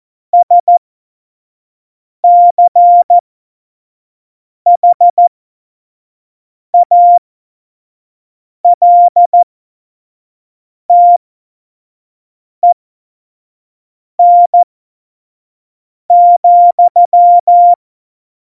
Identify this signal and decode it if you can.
Morse code, heard as a keyed tone, about 14 wpm, spaced as for 5 wpm, 700 Hz. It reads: SCHALTEN,
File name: Morsekurs / Witze / Witz 28 April 2020, 25 F70